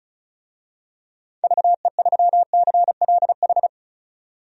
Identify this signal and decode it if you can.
Morse code, heard as a keyed tone, about 34 wpm, 700 Hz. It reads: VE3CLH
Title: DXCC-100 Call Signs - 6 Characters - 35wpm